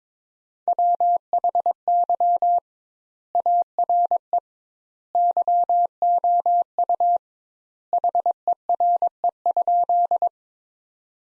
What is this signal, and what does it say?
Morse code, heard as a keyed tone, about 22 wpm, 700 Hz. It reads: WHY ARE YOU HERE?
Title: Sentences Top 200 Words - 22wpm